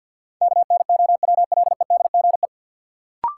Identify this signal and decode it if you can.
Morse code, heard as a keyed tone, about 52 wpm, 700 Hz. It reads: KNOWLEDGE
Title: Sentences from Top 600 Words - New Word Drill - 50wpm